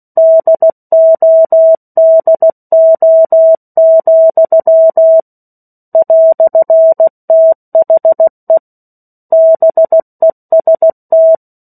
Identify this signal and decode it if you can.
Morse code, heard as a keyed tone, about 16 wpm, 650 Hz. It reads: DODO, "THE BEST
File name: 03-Chapter-03 16 WPM